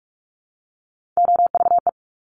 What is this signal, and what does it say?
Morse code, heard as a keyed tone, about 46 wpm, 700 Hz. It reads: OVE
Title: Top Three Letter Combinations in Words 45wpm